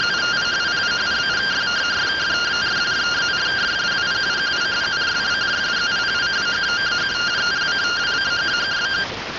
Two-tone teleprinter signal, about 170 Hz shift.